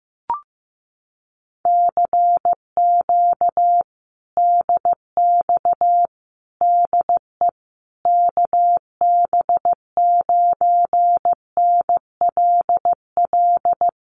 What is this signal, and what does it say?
Morse code, heard as a keyed tone, about 15 wpm, 700 Hz. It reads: CQ DX DE KB9NLL
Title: Common QSO Phrases Encore 15wpm